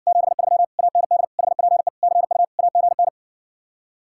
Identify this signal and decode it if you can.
Morse code, heard as a keyed tone, about 60 wpm, 700 Hz. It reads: 73 AND HPE CU AGN